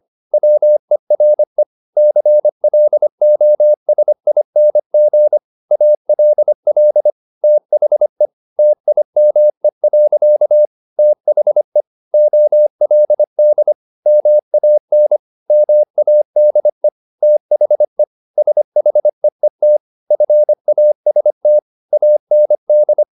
Morse code, heard as a keyed tone, about 25 wpm, 600 Hz. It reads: WERE CLOSING ALL THE TIME. THE OLD MAN MADE THE SHEET FAST AND